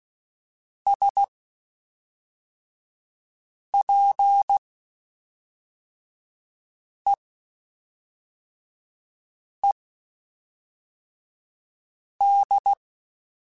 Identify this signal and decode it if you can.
Morse code, heard as a keyed tone, about 16 wpm, spaced as for 3 wpm, 800 Hz. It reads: SPEED
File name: Spaced Characters / Spaced Characters-mp3s / Weather5 22 sp @5